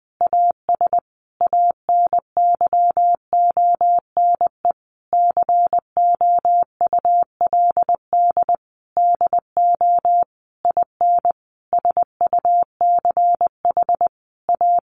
Morse code, heard as a keyed tone, about 20 wpm, 700 Hz. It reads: AS ANYONE COULD DO IN SUCH A
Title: Sentences from Top 1200 Words 20wpm